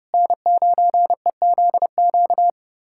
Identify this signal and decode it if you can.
Morse code, heard as a keyed tone, about 30 wpm, 700 Hz. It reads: N9EZQ
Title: USA Callsign DE Callsign - 5 Characters - 30wpm